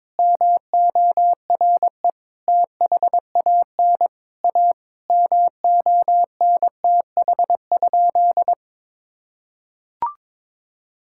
Morse code, heard as a keyed tone, about 22 wpm, 700 Hz. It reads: MORE THAN A MONTH?
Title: Sentences from Top 1000 Words - Review Part 6 - 22wpm